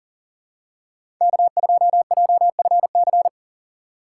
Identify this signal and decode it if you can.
Morse code, heard as a keyed tone, about 39 wpm, 700 Hz. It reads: K2JFC
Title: USA Call Signs - 5 Characters - 4x Character Spacing - 40wpm